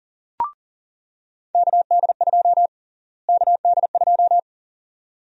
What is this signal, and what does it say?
Morse code, heard as a keyed tone, about 40 wpm, 700 Hz. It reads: KD2 KD2
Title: USA Call Sign Prefixes - Repeated 2x - 40wpm